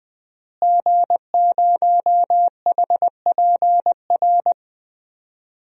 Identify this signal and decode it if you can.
Morse code, heard as a keyed tone, about 21 wpm, 700 Hz. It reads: G0HPR
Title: DXCC-100 Call Signs - 5 Characters - Repeated 3x - 20wpm